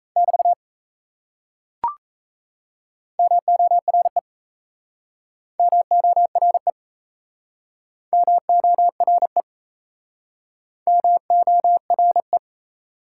Morse code, heard as a keyed tone, about 36 wpm, 700 Hz. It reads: <BT> MORE MORE MORE MORE